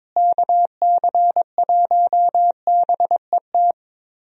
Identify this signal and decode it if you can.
Morse code, heard as a keyed tone, about 22 wpm, 700 Hz. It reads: KC1BET
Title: USA Call Signs - 6 Characters - Repeated 2x - 22wpm